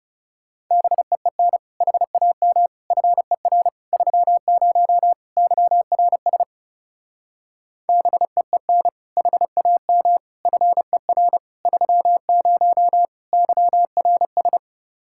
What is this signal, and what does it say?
Morse code, heard as a keyed tone, about 38 wpm, 700 Hz. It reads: BEEN HAM FER 30 YRS BEEN HAM FER 30 YRS